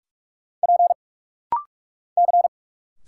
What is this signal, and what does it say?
Morse code, heard as a keyed tone, about 44 wpm, 700 Hz. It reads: P C
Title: Single Letter-Number Alt - Rapid-Fire 45wpm